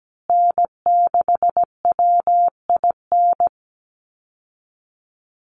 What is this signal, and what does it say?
Morse code, heard as a keyed tone, about 17 wpm, 700 Hz. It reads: N6WIN
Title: Call Signs Repeated 2x - 17wpm